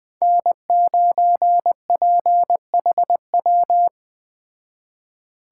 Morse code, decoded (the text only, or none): N9PHW